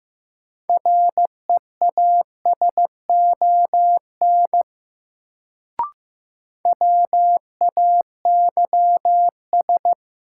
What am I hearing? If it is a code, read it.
Morse code, heard as a keyed tone, about 15 wpm, 700 Hz. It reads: REASON WAYS
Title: Sets of 5 Words Encore - New Word Drill - 15wpm